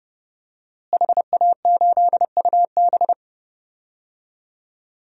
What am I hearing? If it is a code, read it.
Morse code, heard as a keyed tone, about 30 wpm, 700 Hz. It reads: HA8UB
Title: DXCC-100 Call Signs - 5 Characters - Repeated 3x - 30wpm